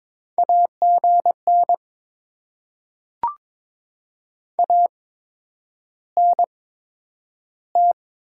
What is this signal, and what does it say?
Morse code, heard as a keyed tone, about 22 wpm, spaced as for 9 wpm, 700 Hz. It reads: AGN ANT
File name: Top 20 QSO Elements - 8x Character Spacing - 22wpm